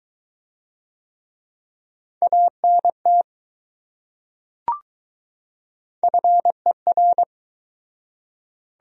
Morse code, heard as a keyed tone, about 21 wpm, 700 Hz. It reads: ANT FER